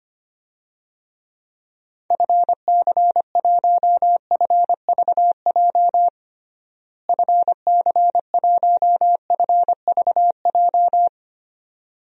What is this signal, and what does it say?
Morse code, heard as a keyed tone, about 25 wpm, 700 Hz. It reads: FC1FVJ FC1FVJ
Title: DXCC-100 Call Signs - 6 Characters - Repeated 2x - 25wpm